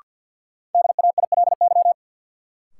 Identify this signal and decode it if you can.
Morse code, heard as a keyed tone, about 46 wpm, 700 Hz. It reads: DAILY
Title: Sentences from Top 1100 Words - New Word Drill Alt - 50wpm